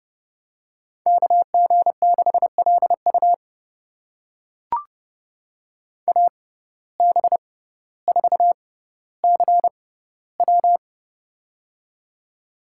Morse code, heard as a keyed tone, about 32 wpm, spaced as for 16 wpm, 700 Hz. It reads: KG6LU AB4CW
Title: USA Call Signs - 5 Characters - 6x Character Spacing - 30wpm